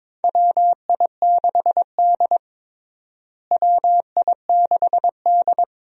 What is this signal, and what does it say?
Morse code, heard as a keyed tone, about 22 wpm, 700 Hz. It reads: WI6D WI6D
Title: USA Call Signs - 4 Characters - Repeated 3x - 22wpm